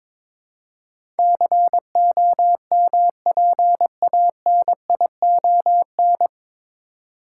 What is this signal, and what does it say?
Morse code, heard as a keyed tone, about 22 wpm, 700 Hz. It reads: COMPANION
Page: Morse Code Ninja practice 22wpm